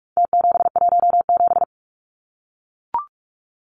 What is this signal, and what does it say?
Morse code, heard as a keyed tone, about 44 wpm, 700 Hz. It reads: T717
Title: Numbers - Contest Style - 4-Digit 45wpm